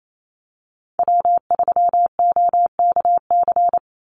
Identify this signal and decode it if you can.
Morse code, heard as a keyed tone, about 28 wpm, 700 Hz. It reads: W3OKC
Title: Call Signs - 28wpm